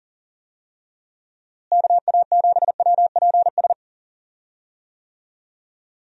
Morse code, heard as a keyed tone, about 39 wpm, 700 Hz. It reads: KA7WPS
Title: USA Call Signs - 6 Characters - Repeated 2x - 40wpm